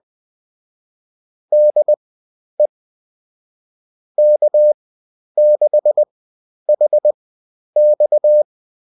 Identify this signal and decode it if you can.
Morse code, heard as a keyed tone, about 20 wpm, spaced as for 10 wpm, 600 Hz. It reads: DE K6HX